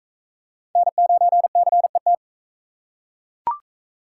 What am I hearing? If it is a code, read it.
Morse code, heard as a keyed tone, about 38 wpm, 700 Hz. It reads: N9CET